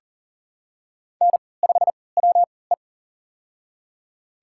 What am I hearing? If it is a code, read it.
Morse code, heard as a keyed tone, about 40 wpm, spaced as for 23 wpm, 700 Hz. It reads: N5WE
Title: USA Call Signs - 4 Characters - 3x Character Spacing - 40wpm